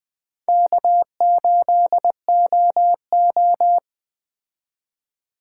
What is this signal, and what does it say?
Morse code, heard as a keyed tone, about 20 wpm, 700 Hz. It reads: K8OO
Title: Speed Racing - USA Call Signs 17wpm